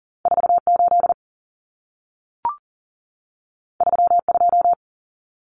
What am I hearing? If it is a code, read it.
Morse code, heard as a keyed tone, about 40 wpm, 700 Hz. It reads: 48 32
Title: Common Cut Numbers - 2-Digit 40wpm